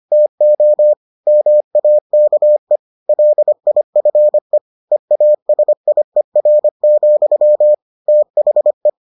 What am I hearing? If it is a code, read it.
Morse code, heard as a keyed tone, about 25 wpm, 600 Hz. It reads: TO MAKE LIFE EASIER, THE